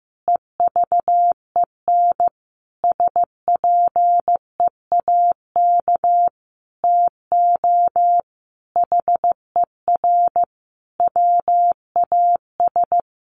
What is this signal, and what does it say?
Morse code, heard as a keyed tone, about 15 wpm, 700 Hz. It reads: EVEN SPEAK TO HER WAS
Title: Sentences from Top 1500 Words 15wpm